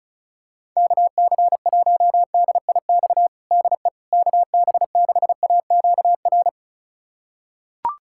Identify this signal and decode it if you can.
Morse code, heard as a keyed tone, about 36 wpm, 700 Hz. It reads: KC1DIX DE KB6AQR
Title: USA Callsign DE Callsign 35wpm